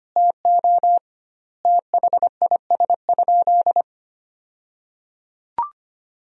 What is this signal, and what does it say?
Morse code, heard as a keyed tone, about 25 wpm, 700 Hz. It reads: TO THIS?
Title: 2x Word-Spacing - Sentences from Top 100 Words - 25wpm